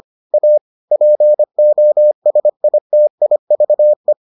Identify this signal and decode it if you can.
Morse code, heard as a keyed tone, about 25 wpm, 600 Hz. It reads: A POSITIVE